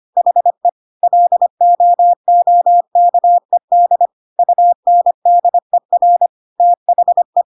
Morse code, heard as a keyed tone, about 25 wpm, 700 Hz. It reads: HE LOOKED UNDER THE
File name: Pinocchio / Pinocchio ch1 25 25 ik1hge